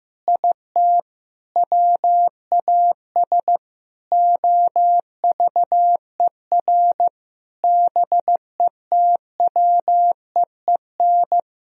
Morse code, heard as a keyed tone, about 15 wpm, 700 Hz. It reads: IT WAS OVER BETWEEN
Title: Sentences from Top 1000 Words - Review Part 1 - 15wpm